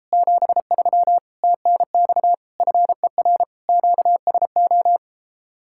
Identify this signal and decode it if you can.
Morse code, heard as a keyed tone, about 33 wpm, 700 Hz. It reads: 73 TNX FER QSO